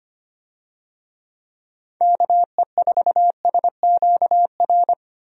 Morse code, heard as a keyed tone, about 25 wpm, 700 Hz. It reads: KE4SQR
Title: USA Call Signs - 6 Characters - 25wpm